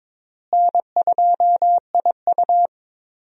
Morse code, heard as a keyed tone, about 22 wpm, 700 Hz. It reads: N2IU
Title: USA Call Signs - 4 Characters - Repeated 2x - 22wpm